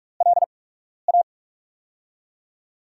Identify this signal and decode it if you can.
Morse code, heard as a keyed tone, about 44 wpm, spaced as for 12 wpm, 690 Hz. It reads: LA
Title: USA Call Signs - 4 Characters - 8x Character Spacing - 45wpm